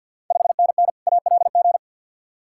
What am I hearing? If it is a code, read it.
Morse code, heard as a keyed tone, about 52 wpm, 690 Hz. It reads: 5NN ALG
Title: Ontario QSO Party Exchange 50wpm